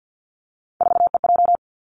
Morse code, heard as a keyed tone, about 50 wpm, 700 Hz. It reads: 4EJ
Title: Three Challenging Letter-Number Combinations 50wpm